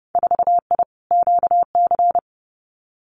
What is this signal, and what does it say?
Morse code, heard as a keyed tone, about 30 wpm, 700 Hz. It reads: 4I QC
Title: Winter Field Day Exchanges Encore 30wpm